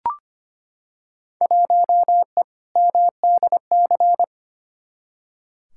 Morse code, heard as a keyed tone, about 25 wpm, 700 Hz. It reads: 1E MDC
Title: Field Day Exchanges Encore - 25wpm